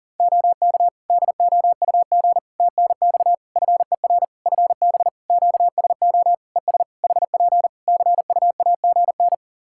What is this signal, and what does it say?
Morse code, heard as a keyed tone, about 40 wpm, 700 Hz. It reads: OK DOUG TNX FER FB QSO ES HP CUAGN
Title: Practice QSO - Exchange 4 - Ending - 40wpm